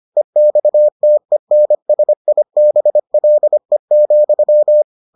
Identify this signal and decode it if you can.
Morse code, heard as a keyed tone, about 25 wpm, 600 Hz. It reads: EXTENSIBLE,